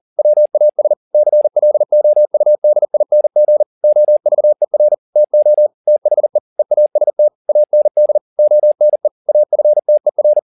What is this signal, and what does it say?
Morse code, heard as a keyed tone, about 40 wpm, 600 Hz. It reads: WAS CLOUDING OVER TO THE EAST AND ONE AFTER
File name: the old man and the sea 2 40wpm